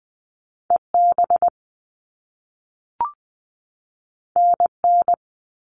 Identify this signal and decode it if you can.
Morse code, heard as a keyed tone, about 20 wpm, 700 Hz. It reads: E B N N
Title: All Cut Numbers - 2-Digit 20wpm